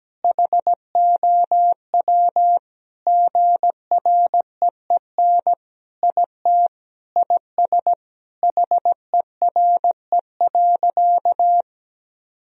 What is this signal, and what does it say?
Morse code, heard as a keyed tone, about 17 wpm, 700 Hz. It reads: HOW GREEN IT IS HERE.
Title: Sentences from Top 1000 Words - Review Part 2 - 17wpm